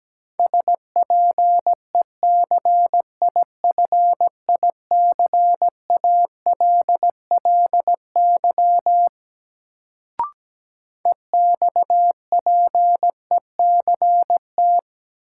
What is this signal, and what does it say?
Morse code, heard as a keyed tone, about 17 wpm, 700 Hz. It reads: SPECIFICALLY EXPECT
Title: Sentences from Top 1100 Words - New Word Drill - 17wpm